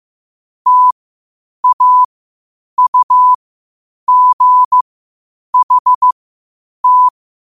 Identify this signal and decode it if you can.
Morse code, heard as a keyed tone, about 15 wpm, spaced as for 8 wpm, 1000 Hz. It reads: TAUGHT
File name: qotdfiles / 8wpm